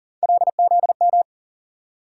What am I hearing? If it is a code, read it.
Morse code, heard as a keyed tone, about 40 wpm, 700 Hz. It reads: LZM
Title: USA Call Sign Postfixes - 3 Letters 40wpm